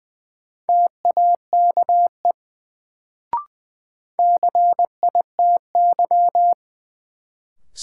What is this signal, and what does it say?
Morse code, heard as a keyed tone, about 20 wpm, 700 Hz. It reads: TAKE CITY
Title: Sets of 2 Words - New Word Drill - 20wpm